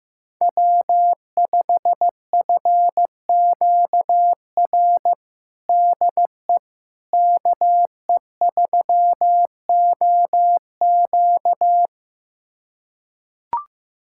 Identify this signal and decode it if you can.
Morse code, heard as a keyed tone, about 15 wpm, 700 Hz. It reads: W5FQR DE KE3OQ
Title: USA Callsign DE Callsign - 5 Characters - 15wpm